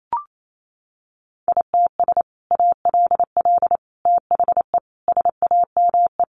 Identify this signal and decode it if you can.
Morse code, heard as a keyed tone, about 28 wpm, 700 Hz. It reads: ITS ALL THE SAME
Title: Sentences from Top 600 Words 28wpm